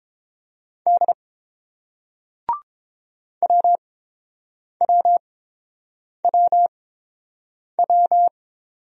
Morse code, heard as a keyed tone, about 27 wpm, spaced as for 7 wpm, 700 Hz. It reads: D WWWW
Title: Speed-Racing Single Letters 22wpm